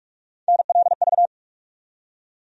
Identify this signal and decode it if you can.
Morse code, heard as a keyed tone, about 47 wpm, 700 Hz. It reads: NLV
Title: USA Call Sign Postfixes - 3 Letters 45wpm